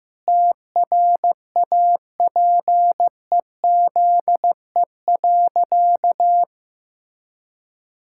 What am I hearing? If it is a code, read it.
Morse code, heard as a keyed tone, about 15 wpm, 700 Hz. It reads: TRAPEZE.
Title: Pangram Challenge v2 - 15wpm